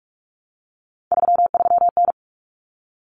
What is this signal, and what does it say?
Morse code, heard as a keyed tone, about 44 wpm, 700 Hz. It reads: 33N
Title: Common Cut Numbers - 3-Digit 45wpm